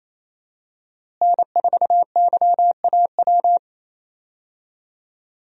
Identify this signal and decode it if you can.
Morse code, heard as a keyed tone, about 28 wpm, 700 Hz. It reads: N4YAW
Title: USA Call Signs - 5 Characters - Repeated 3x - 28wpm